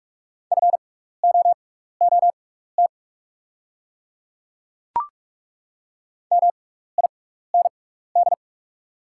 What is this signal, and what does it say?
Morse code, heard as a keyed tone, about 43 wpm, spaced as for 15 wpm, 700 Hz. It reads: FOOT MIND
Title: 6x Character Spacing - Top 500 Words - 45wpm